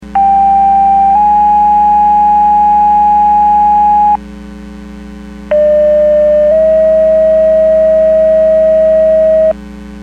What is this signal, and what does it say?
Two-tone page, heard as a one-second tone, then a three-second tone, about 790 then 830 Hz, about 600 then 640 Hz. It is heard from 0.2 to 9.5 s.
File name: Portage 154.385 Paging